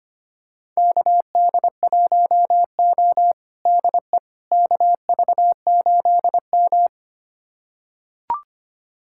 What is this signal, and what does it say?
Morse code, heard as a keyed tone, about 25 wpm, 700 Hz. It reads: KD1O DE KV8M
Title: USA Callsign DE Callsign - 4 Characters - 25wpm